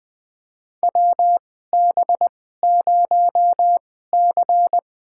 Morse code, heard as a keyed tone, about 20 wpm, spaced as for 14 wpm, 700 Hz. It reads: WB0C
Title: USA Call Signs - 2x Character Spacing - 20wpm